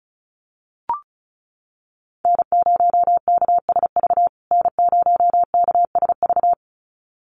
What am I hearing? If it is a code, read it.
Morse code, heard as a keyed tone, about 34 wpm, 700 Hz. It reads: N0KSV N0KSV
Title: USA Call Signs Repeated 2x 35wpm